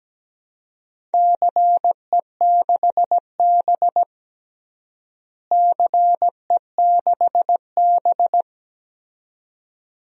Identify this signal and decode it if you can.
Morse code, heard as a keyed tone, about 17 wpm, 700 Hz. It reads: CE6B CE6B
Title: DXCC-100 Call Signs - 4 Characters - Repeated 2x - 17wpm